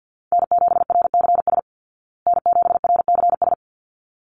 Morse code, heard as a keyed tone, about 50 wpm, 700 Hz. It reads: N7RCS N7RCS
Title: Call Signs Repeated 3x - 50wpm